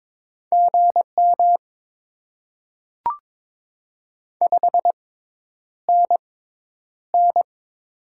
Morse code, heard as a keyed tone, about 22 wpm, spaced as for 10 wpm, 690 Hz. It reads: GM 5NN
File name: Top 20 QSO Elements - 6x Character Spacing - 22wpm